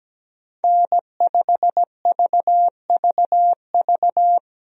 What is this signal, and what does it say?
Morse code, heard as a keyed tone, about 17 wpm, 700 Hz. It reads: N5VVV
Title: USA Call Signs - 5 Characters - 17wpm